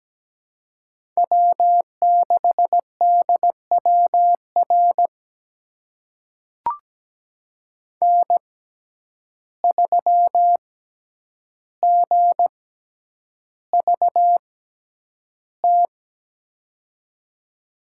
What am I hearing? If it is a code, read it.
Morse code, heard as a keyed tone, about 17 wpm, spaced as for 9 wpm, 700 Hz. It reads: W6DWR N3GVT
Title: USA Call Signs - 5 Characters - 6x Character Spacing - 17wpm